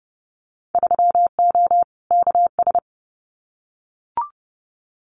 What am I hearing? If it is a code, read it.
Morse code, heard as a keyed tone, about 30 wpm, 700 Hz. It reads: 3O KS
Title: Winter Field Day Exchanges 30wpm